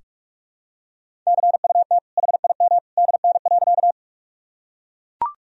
Morse code, heard as a keyed tone, about 41 wpm, 700 Hz. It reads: CUT HIM DN.
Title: Sentences from Top 300 Words and Abbreviations 45wpm